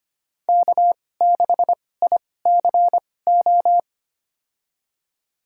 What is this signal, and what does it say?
Morse code, heard as a keyed tone, about 25 wpm, spaced as for 18 wpm, 700 Hz. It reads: K6ICO
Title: USA Call Signs - 5 Characters - 2x Character Spacing - 25wpm